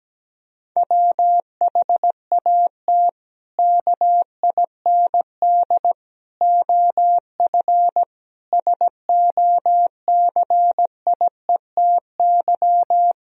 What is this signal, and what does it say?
Morse code, heard as a keyed tone, about 17 wpm, 700 Hz. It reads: WHAT KIND OF SOCIETY
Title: Sentences from Top 1000 Words - Review Part 4 - 17wpm